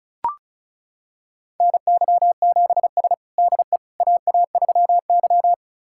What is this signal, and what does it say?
Morse code, heard as a keyed tone, about 34 wpm, 700 Hz. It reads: NY7S DE AA3Y
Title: USA Callsign DE Callsign - 4 Characters - 35wpm